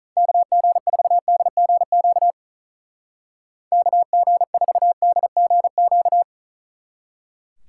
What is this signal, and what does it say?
Morse code, heard as a keyed tone, about 44 wpm, 700 Hz. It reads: KG4DGQ KG4DGQ
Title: Speed Racing - USA Call Signs 35wpm